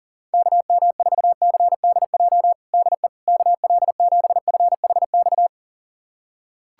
Morse code, heard as a keyed tone, about 40 wpm, 700 Hz. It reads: KM4CDJ DE KL7FHX
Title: USA Callsign DE Callsign - 6 Characters - 40wpm